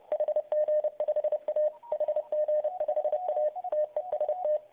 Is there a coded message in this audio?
Yes: Morse code from 0.1 to 4.6 s, heard as a keyed tone, about 30 wpm, 590 Hz, with radio noise. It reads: HG5A HG5A TEST